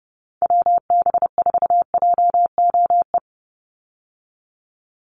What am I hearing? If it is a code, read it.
Morse code, heard as a keyed tone, about 29 wpm, 700 Hz. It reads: WB4JOE
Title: USA Call Signs 30wpm